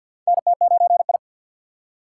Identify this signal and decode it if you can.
Morse code, heard as a keyed tone, about 47 wpm, 700 Hz. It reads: NT9I
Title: USA Call Signs - 2x Character Spacing - 50wpm